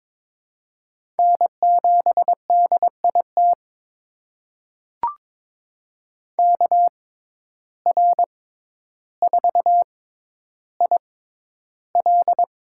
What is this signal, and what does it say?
Morse code, heard as a keyed tone, about 22 wpm, spaced as for 11 wpm, 700 Hz. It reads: N7DIT KR4IL
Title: USA Call Signs - 5 Characters - 6x Character Spacing - 22wpm